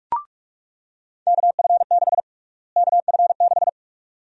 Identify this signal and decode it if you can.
Morse code, heard as a keyed tone, about 42 wpm, 690 Hz. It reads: KF6 KF6
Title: USA Call Sign Prefixes - Repeated 2x - 45wpm